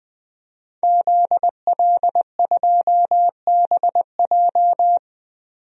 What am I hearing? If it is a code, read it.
Morse code, heard as a keyed tone, about 20 wpm, 700 Hz. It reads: ZL2BJ
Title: DXCC-100 Call Signs - 5 Characters - 20wpm